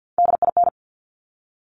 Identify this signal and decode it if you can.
Morse code, heard as a keyed tone, about 50 wpm, 700 Hz. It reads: DIN
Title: Top Three Letter Combinations in Words 50wpm